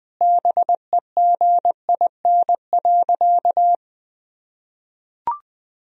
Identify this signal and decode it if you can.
Morse code, heard as a keyed tone, about 20 wpm, 700 Hz. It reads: BEGIN.
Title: Sentences from Top 1000 Words - Review Part 1 - 20wpm